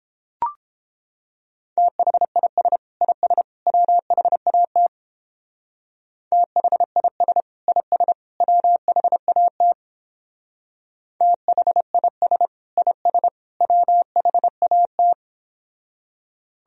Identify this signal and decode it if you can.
Morse code, heard as a keyed tone, about 29 wpm, 700 Hz. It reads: THIS IS WHAT THIS IS WHAT THIS IS WHAT